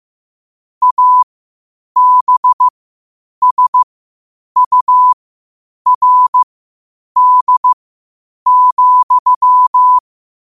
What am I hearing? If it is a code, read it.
Morse code, heard as a keyed tone, about 15 wpm, spaced as for 8 wpm, 1000 Hz. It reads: ABSURD,
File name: qotdfiles / 8wpm